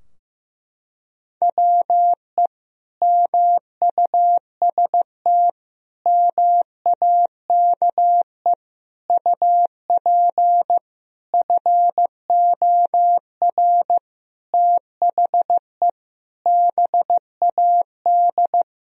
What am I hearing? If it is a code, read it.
Morse code, heard as a keyed tone, about 15 wpm, 700 Hz. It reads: WE MUST MAKE UP FOR THE BAD
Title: Sentences from Top 1000 Words - Review Part 3 - 15wpm